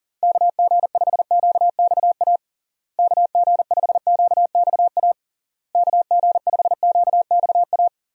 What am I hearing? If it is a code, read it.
Morse code, heard as a keyed tone, about 40 wpm, 700 Hz. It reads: KG5QXA KG5QXA KG5QXA
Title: USA Call Signs - 6 Characters - Repeated 3x - 40wpm